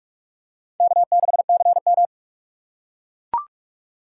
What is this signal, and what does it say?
Morse code, heard as a keyed tone, about 44 wpm, 700 Hz. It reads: K6CM
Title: USA Call Signs - 4 Characters - 3x Character Spacing - 45wpm